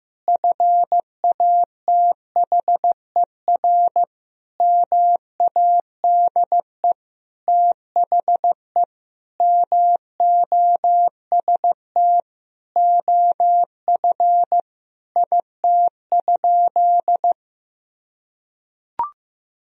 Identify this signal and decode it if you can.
Morse code, heard as a keyed tone, about 16 wpm, 700 Hz. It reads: FATHER MADE THE MOST OF IT?
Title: Sentences from Top 1000 Words - Review Part 1 - 15wpm